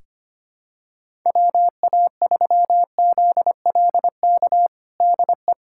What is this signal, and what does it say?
Morse code, heard as a keyed tone, about 25 wpm, 700 Hz. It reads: WA3ZLK DE
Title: USA Callsign DE Callsign 25wpm